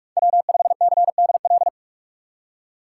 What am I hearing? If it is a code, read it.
Morse code, heard as a keyed tone, about 45 wpm, 700 Hz. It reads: W5CDL
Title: USA Call Signs - 5 Characters - 2x Character Spacing - 45wpm